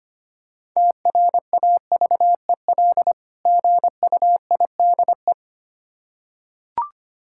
Morse code, heard as a keyed tone, about 25 wpm, 700 Hz. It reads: TRAVEL GUIDE
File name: practice / 25wpm / Difficult Words with Hints - 25wpm